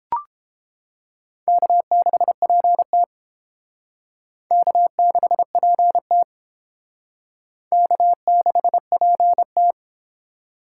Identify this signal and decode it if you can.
Morse code, heard as a keyed tone, about 30 wpm, 700 Hz. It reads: K6PT K6PT K6PT